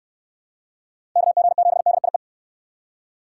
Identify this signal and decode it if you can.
Morse code, heard as a keyed tone, about 62 wpm, 700 Hz. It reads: KG7DIE